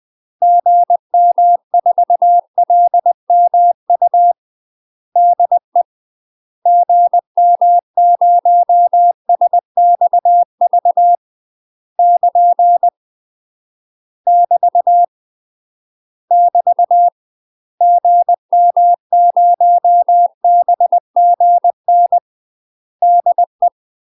Morse code, heard as a keyed tone, about 20 wpm, 700 Hz. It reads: GM4LMU DE GM0SXV <KN> <BT> <BT> GM0BGN DE
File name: UK MASTER QSOs TRAINING/20WPM